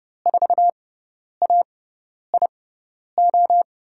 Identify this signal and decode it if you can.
Morse code, heard as a keyed tone, about 31 wpm, spaced as for 10 wpm, 700 Hz. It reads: 4AIO
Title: USA Call Signs - 6x Character Spacing - 30wpm